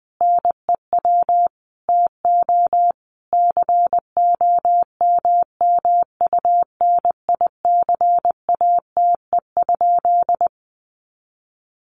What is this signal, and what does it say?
Morse code, heard as a keyed tone, about 20 wpm, 700 Hz. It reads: NEW TO COMMUNICATE?
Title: Sentences from Top 1300 Words 20wpm